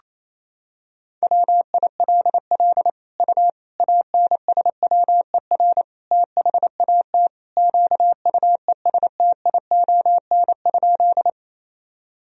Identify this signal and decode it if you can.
Morse code, heard as a keyed tone, about 28 wpm, 700 Hz. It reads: WILL U ANSWER THAT QUESTION?
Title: Sentences from Top 400 Words and Abbreviations 28wpm